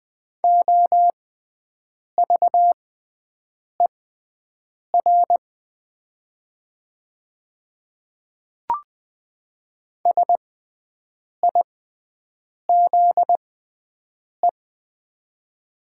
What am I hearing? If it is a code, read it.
Morse code, heard as a keyed tone, about 20 wpm, spaced as for 7 wpm, 700 Hz. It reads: OVER SIZE
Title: 6x Character Spacing - Top 500 Words - 20wpm